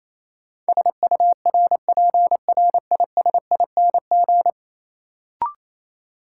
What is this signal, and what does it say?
Morse code, heard as a keyed tone, about 28 wpm, 700 Hz. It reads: SURPRISING
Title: Sets of 5 Words - New Word Drill - 28wpm